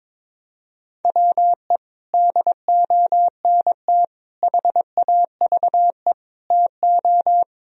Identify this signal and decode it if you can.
Morse code, heard as a keyed tone, about 22 wpm, 700 Hz. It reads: WE DONT HAVE TO
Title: Sentences from Top 1000 Words - Review Part 2 - 22wpm